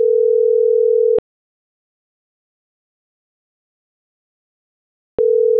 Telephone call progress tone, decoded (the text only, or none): ringback tone